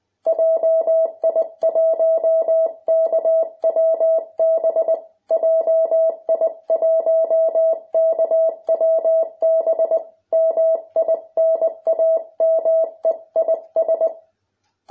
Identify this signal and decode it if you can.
Morse code, heard as a keyed tone, about 19 wpm, 650 Hz. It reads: JI1KWB JI1KWB MI NAME IS